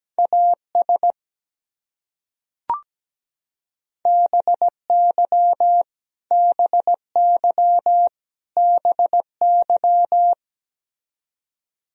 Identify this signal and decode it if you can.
Morse code, heard as a keyed tone, about 17 wpm, 700 Hz. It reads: AS BY BY BY
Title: Top 100 Words Repeated 17wpm